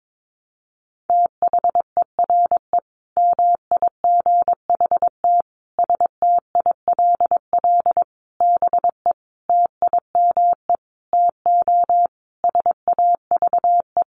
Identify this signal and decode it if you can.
Morse code, heard as a keyed tone, about 22 wpm, 700 Hz. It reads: THERE MIGHT STILL BE TIME TO SAVE
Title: Sentences from Top 1100 Words 22wpm